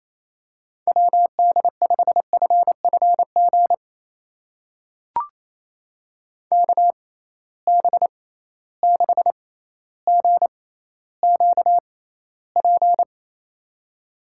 Morse code, heard as a keyed tone, about 28 wpm, spaced as for 14 wpm, 700 Hz. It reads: WD5FFG KB6GQP
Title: USA Call Signs - 6 Characters - 6x Character Spacing - 28wpm